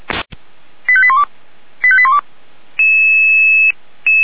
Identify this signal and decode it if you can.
Five-tone selective call (ZVEI), heard as quick runs of five tones, tones of about 70 ms. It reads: 85812 85812